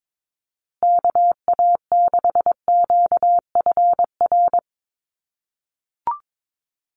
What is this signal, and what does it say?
Morse code, heard as a keyed tone, about 22 wpm, 700 Hz. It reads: KA6QFR
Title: USA Call Signs Repeated 3x 22wpm